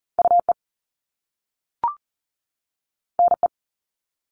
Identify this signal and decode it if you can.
Morse code, heard as a keyed tone, about 38 wpm, 700 Hz. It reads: UE NE
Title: Top Two Letter Combinations in Words 40wpm